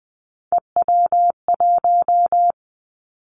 Morse code, heard as a keyed tone, about 20 wpm, 700 Hz. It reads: EW1
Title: Callsign Prefixes for Top 100 DXCC - 20wpm